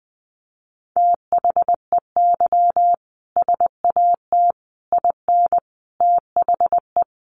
Morse code, heard as a keyed tone, about 20 wpm, 700 Hz. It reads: THEY SAT IN THE
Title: A Sound Of Thunder 20wpm